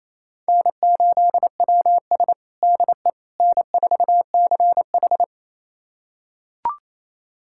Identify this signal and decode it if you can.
Morse code, heard as a keyed tone, about 28 wpm, 700 Hz. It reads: N8WS DE N4CH
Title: USA Callsign DE Callsign - 4 Characters - 28wpm